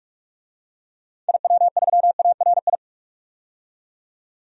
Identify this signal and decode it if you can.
Morse code, heard as a keyed tone, about 46 wpm, 700 Hz. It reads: IW3ARI
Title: DXCC-100 Call Signs - 6 Characters - 45wpm